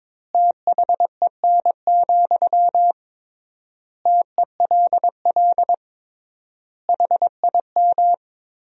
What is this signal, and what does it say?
Morse code, heard as a keyed tone, about 22 wpm, 700 Hz. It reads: THEN, TELL HIM
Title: 3x Word-Spacing - Sentences from Top 200 Words - 22wpm